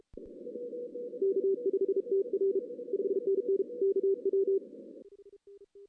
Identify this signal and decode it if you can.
Morse code, heard as a keyed tone, about 33 wpm, 380 Hz. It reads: K5TR 5NN KW
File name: 2016 arrl dx cw / 2016-ARRL-DX-CW-K5TR- - 0-3